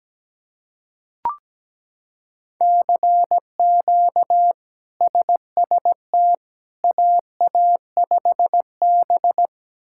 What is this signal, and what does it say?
Morse code, heard as a keyed tone, about 17 wpm, 700 Hz. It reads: CQ SST AA5B
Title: K1USN SST QSO Practice - 17wpm